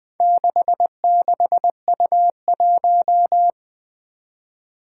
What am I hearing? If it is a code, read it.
Morse code, heard as a keyed tone, about 20 wpm, 700 Hz. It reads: 66U1
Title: Four Challenging Letter-Number Combinations 20wpm